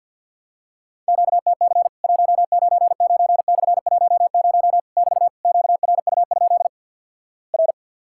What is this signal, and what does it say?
Morse code, heard as a keyed tone, about 50 wpm, 700 Hz. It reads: QTC 199/10 <BT> QRU? R
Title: Worked All Europe DX Contest - QTC Exchange with Repeats - 50wpm